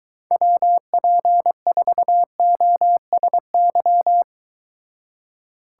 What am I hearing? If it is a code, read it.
Morse code, heard as a keyed tone, about 22 wpm, 700 Hz. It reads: WP4OSY